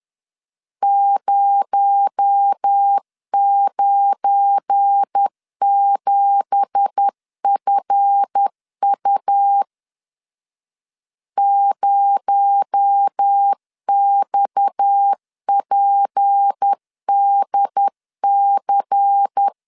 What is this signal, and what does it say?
Morse code, heard as a keyed tone, about 10 wpm, 800 Hz. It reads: 097FU 0XPDC
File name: CW / DET / 1 9 1500C S A 10wpm 4511